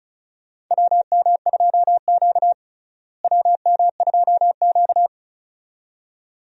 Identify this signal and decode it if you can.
Morse code, heard as a keyed tone, about 36 wpm, 700 Hz. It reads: WM2Q WM2Q
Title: USA Call Signs - 4 Characters - Repeated 2x - 35wpm